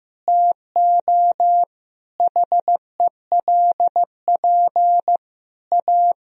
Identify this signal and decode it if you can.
Morse code, heard as a keyed tone, about 15 wpm, 700 Hz. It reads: TO HELP A
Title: Sentences from Top 1000 Words - Review Part 2 - 15wpm